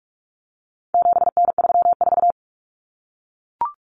Morse code, heard as a keyed tone, about 44 wpm, 700 Hz. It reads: 7N34
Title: Common Cut Numbers - 4-Digit 45wpm